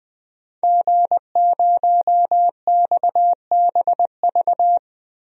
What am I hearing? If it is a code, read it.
Morse code, heard as a keyed tone, about 21 wpm, 700 Hz. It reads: G0XBV
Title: DXCC-100 Call Signs - 5 Characters - 20wpm